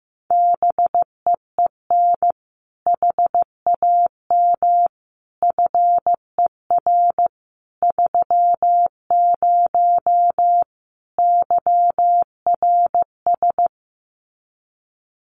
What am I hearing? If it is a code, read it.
Morse code, heard as a keyed tone, about 15 wpm, 700 Hz. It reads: BEEN HAM FER 30 YRS
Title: Common QSO Phrases Encore 15wpm